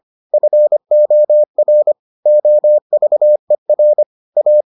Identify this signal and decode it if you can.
Morse code, heard as a keyed tone, about 25 wpm, 600 Hz. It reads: FOR OVER A